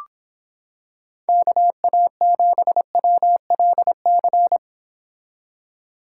Morse code, heard as a keyed tone, about 26 wpm, 700 Hz. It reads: KA7WLC